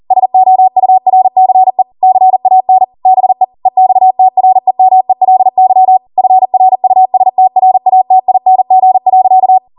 Morse code, heard as a keyed tone, about 40 wpm, 750 Hz. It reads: SOURCE CAN BE EXTREMELY FRUSTRATING.
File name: Morse / 40WPM